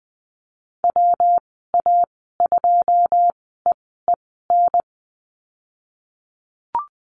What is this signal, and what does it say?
Morse code, heard as a keyed tone, about 20 wpm, spaced as for 15 wpm, 700 Hz. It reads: WA2EEN
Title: 154 Lesson 25g - Letter p - Call Signs 20wpm